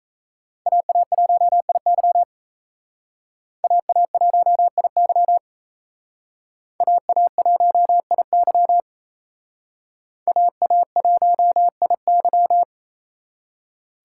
Morse code, heard as a keyed tone, about 36 wpm, 700 Hz. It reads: AA1IY AA1IY AA1IY AA1IY